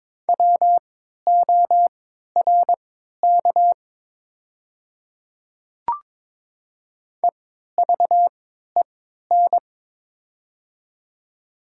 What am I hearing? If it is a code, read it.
Morse code, heard as a keyed tone, about 22 wpm, spaced as for 12 wpm, 700 Hz. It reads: WORK EVEN
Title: 3x Character Spacing - Top 200 Words - 22wpm